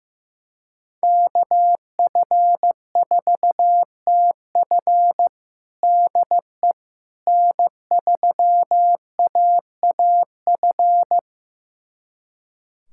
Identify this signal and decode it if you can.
Morse code, heard as a keyed tone, about 15 wpm, 700 Hz. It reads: KF4TF DE N3AAF
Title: USA Callsign DE Callsign - 5 Characters - 15wpm